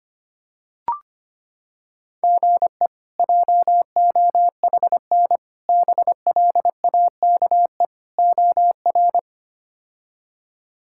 Morse code, heard as a keyed tone, about 25 wpm, 700 Hz. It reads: GE JOHN BLAKE OR
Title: K1USN SST QSO Practice - 25wpm